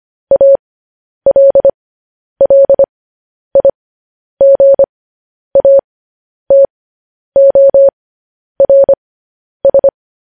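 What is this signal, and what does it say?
Morse code, heard as a keyed tone, about 25 wpm, spaced as for 10 wpm, 550 Hz. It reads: ALLIGATORS
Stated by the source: Morse code news headlines.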